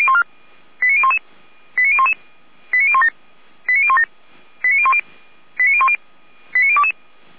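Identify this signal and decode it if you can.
Five-tone selective call (ZVEI), heard as quick runs of five tones, tones of about 70 ms. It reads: 89011 89011 89017 89017 89010 89010 89022
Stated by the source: Five-tone ZVEI selective call 89001, received on SDR, from Feuerwehr Orlamünde.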